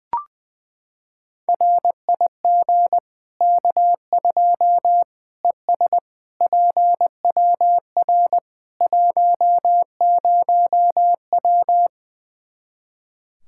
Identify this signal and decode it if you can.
Morse code, heard as a keyed tone, about 20 wpm, 700 Hz. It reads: RIG K2 ES PWR 10W
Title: Practice QSO - Exchange 2 - Info - 20wpm